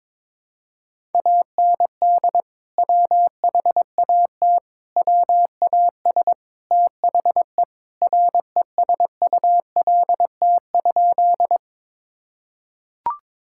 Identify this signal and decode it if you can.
Morse code, heard as a keyed tone, about 22 wpm, 700 Hz. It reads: AND WHAT WAS THE RESULT?
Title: Sentences from Top 1000 Words - Review Part 5 - 22wpm